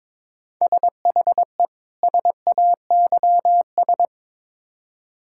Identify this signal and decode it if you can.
Morse code, heard as a keyed tone, about 22 wpm, 700 Hz. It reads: SHE SAYS
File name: Build A Phrase - 2-Word Phrases - 22wpm